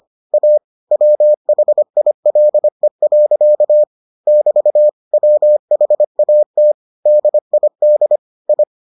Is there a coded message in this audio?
Yes: Morse code from 0.3 to 8.6 s, heard as a keyed tone, about 25 wpm, 600 Hz. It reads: A WHILE. <BT> WHAT DID I